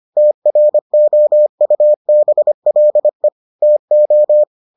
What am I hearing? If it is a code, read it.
Morse code, heard as a keyed tone, about 25 wpm, 600 Hz. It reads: TROUBLE TO